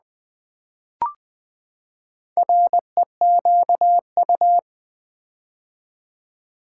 Morse code, heard as a keyed tone, about 20 wpm, 700 Hz. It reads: REQU
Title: Build A Word - 4 Letters At A Time - 11-Letter Words - 20wpm